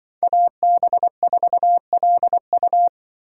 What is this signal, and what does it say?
Morse code, heard as a keyed tone, about 24 wpm, 700 Hz. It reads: AB4LU
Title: Speed Racing - USA 2x2 Call Signs 20wpm